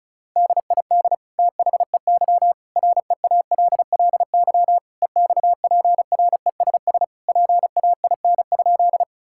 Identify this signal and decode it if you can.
Morse code, heard as a keyed tone, about 36 wpm, 700 Hz. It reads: DID THEY REALLY EXPRESS PAIN?
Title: Sentences from Top 1600 Words 35wpm